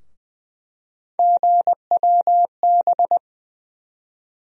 Morse code, heard as a keyed tone, about 20 wpm, 700 Hz. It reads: GWB
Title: USA Call Sign Postfixes - 3 Letters 20wpm